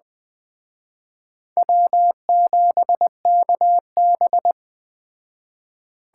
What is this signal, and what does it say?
Morse code, heard as a keyed tone, about 21 wpm, 700 Hz. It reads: W7KB